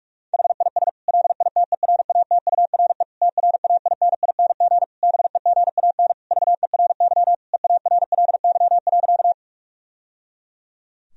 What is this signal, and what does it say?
Morse code, heard as a keyed tone, about 46 wpm, 700 Hz. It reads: HIS LITERATURE TRAINING BEGAN VERY EARLY.
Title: Sentences from Top 1000 Words - Review Part 4 - 45wpm